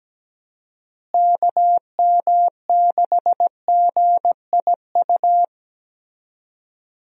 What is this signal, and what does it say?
Morse code, heard as a keyed tone, about 17 wpm, 700 Hz. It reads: KM6GIU
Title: USA Call Signs - 6 Characters - Repeated 2x - 17wpm